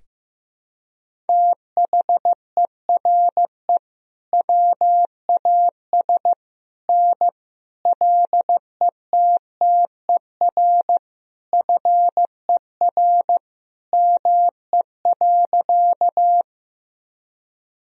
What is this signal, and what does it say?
Morse code, heard as a keyed tone, about 15 wpm, 700 Hz. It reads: THERE WAS N LETTER FER ME.
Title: Sentences from Top 300 Words and Abbreviations 15wpm